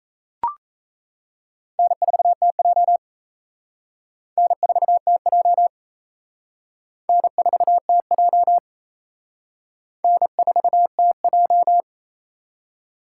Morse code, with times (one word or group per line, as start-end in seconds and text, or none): 1.8-3.0 N4TJ
4.4-5.7 N4TJ
7.1-8.6 N4TJ
10.0-11.8 N4TJ